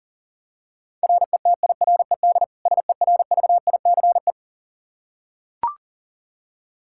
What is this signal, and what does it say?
Morse code, heard as a keyed tone, about 40 wpm, 700 Hz. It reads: RETIRED SERVICE
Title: K1USN SST QSO Practice 40wpm